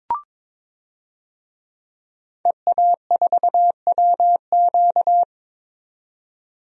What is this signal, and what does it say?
Morse code, heard as a keyed tone, about 22 wpm, 700 Hz. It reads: EA4WQ
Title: DXCC-100 Call Signs - 5 Characters - 22wpm